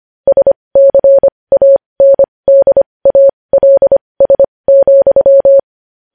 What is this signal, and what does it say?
Morse code, heard as a keyed tone, about 25 wpm, spaced as for 20 wpm, 550 Hz. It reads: SCANDALS,
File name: podcast / News headlines-1x-25x20